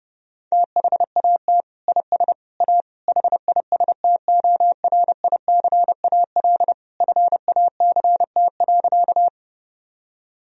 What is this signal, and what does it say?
Morse code, heard as a keyed tone, about 30 wpm, 700 Hz. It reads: THAT IS A HISTORICAL FACT.
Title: Sentences from Top 1000 Words - Review Part 4 - 30wpm